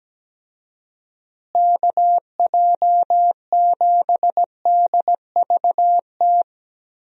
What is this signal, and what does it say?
Morse code, heard as a keyed tone, about 17 wpm, 700 Hz. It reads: KJ7DVT
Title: USA Call Signs - 6 Characters - Repeated 3x - 17wpm